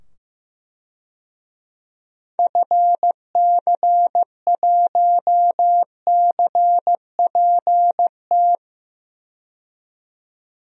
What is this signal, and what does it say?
Morse code, heard as a keyed tone, about 15 wpm, 700 Hz. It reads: FC1CPT
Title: DXCC-100 Call Signs - 6 Characters - 15wpm